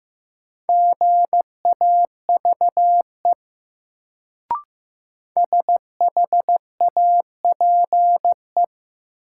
Morse code, heard as a keyed tone, about 15 wpm, 700 Hz. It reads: GAVE SHAPE
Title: Sentences from Top 500 Words - New Word Drill - 15wpm